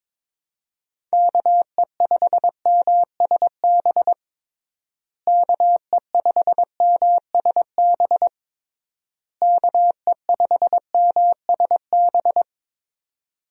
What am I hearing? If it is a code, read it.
Morse code, heard as a keyed tone, about 22 wpm, 700 Hz. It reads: KE5MSB KE5MSB KE5MSB
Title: USA Call Signs - 6 Characters - Repeated 3x - 22wpm